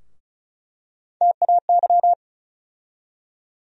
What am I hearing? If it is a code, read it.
Morse code, heard as a keyed tone, about 34 wpm, 700 Hz. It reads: TAY
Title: USA Call Sign Postfixes - 3 Letters 35wpm